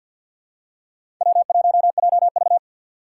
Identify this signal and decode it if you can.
Morse code, heard as a keyed tone, about 48 wpm, 700 Hz. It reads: W1JV
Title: USA Call Signs - 4 Characters - 3x Character Spacing - 50wpm